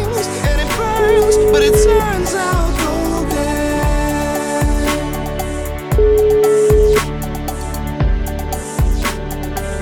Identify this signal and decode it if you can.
Ringback tone (Europe, 425 Hz), with noise.